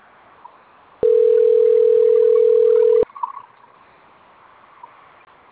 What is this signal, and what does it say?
Ringback tone (North America), 440 + 480 Hz.
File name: rg-1600-unknown- - - .25667